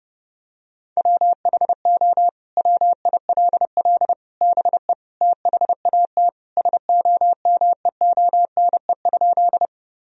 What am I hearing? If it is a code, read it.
Morse code, heard as a keyed tone, about 28 wpm, 700 Hz. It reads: WHO WILL BE THAT SOMEONE?
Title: Sentences from Top 1000 Words - Review Part 4 - 30wpm